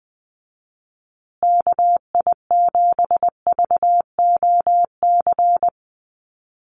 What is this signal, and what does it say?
Morse code, heard as a keyed tone, about 19 wpm, 700 Hz. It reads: KI7VOC
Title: USA Call Signs Repeated 2x 20wpm